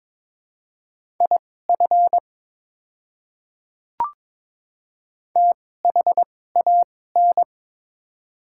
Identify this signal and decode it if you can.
Morse code, heard as a keyed tone, about 23 wpm, spaced as for 16 wpm, 700 Hz. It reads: IF THAN
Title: 2x Character Spacing - Top 100 Words - 22wpm